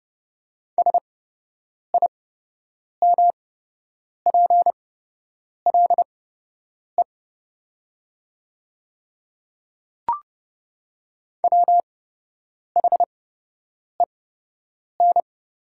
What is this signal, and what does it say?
Morse code, heard as a keyed tone, about 30 wpm, spaced as for 8 wpm, 700 Hz. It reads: SIMPLE WHEN
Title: 8x Character Spacing - Top 500 Words - 30wpm